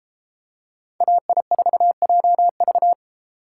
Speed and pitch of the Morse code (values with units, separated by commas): 33 wpm, 700 Hz